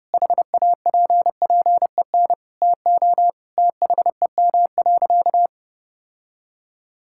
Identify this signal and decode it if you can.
Morse code, heard as a keyed tone, about 30 wpm, 700 Hz. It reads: HAPPEN TO THEM.
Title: Sentences from Top 1000 Words - Review Part 5 - 30wpm